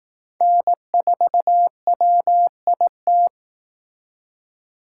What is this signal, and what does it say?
Morse code, heard as a keyed tone, about 18 wpm, 700 Hz. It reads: N4WIT
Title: Speed Racing - USA Call Signs 15wpm